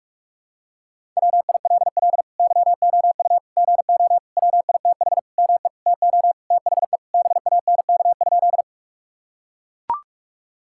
Morse code, heard as a keyed tone, about 43 wpm, 700 Hz. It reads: WILL YOU GO WITH ME TO THE BANK?
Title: Sentences from Top 1000 Words - Review Part 6 - 45wpm